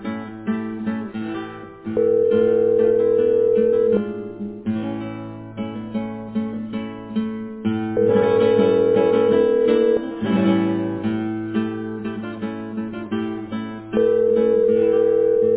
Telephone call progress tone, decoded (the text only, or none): ringback tone